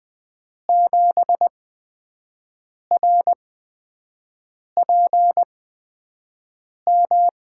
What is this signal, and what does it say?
Morse code, heard as a keyed tone, about 20 wpm, spaced as for 5 wpm, 700 Hz. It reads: 7RPM
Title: USA Call Signs - 5 Characters - 8x Character Spacing - 20wpm